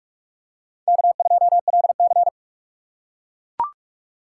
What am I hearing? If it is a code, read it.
Morse code, heard as a keyed tone, about 43 wpm, 700 Hz. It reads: K2LC
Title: USA Call Signs - 4 Characters - 2x Character Spacing - 45wpm